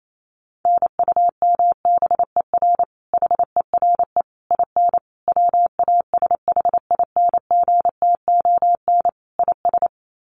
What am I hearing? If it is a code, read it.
Morse code, heard as a keyed tone, about 28 wpm, 700 Hz. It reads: NUMBER HERE IN WASHINGTON IS
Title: Sets of 5 Words Encore2 28wpm